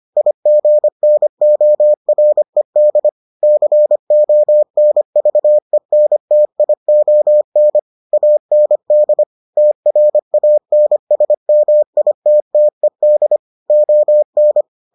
Morse code, heard as a keyed tone, about 25 wpm, 600 Hz. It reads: IGNORED CONVENTION AND TRANSMITTED ON